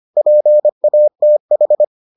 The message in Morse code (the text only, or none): PATH